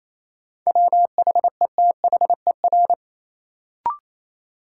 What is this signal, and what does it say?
Morse code, heard as a keyed tone, about 28 wpm, 700 Hz. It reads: WHETHER
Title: Sentences from Top 1000 Words - Review Part 3 - New Word Drill - 28wpm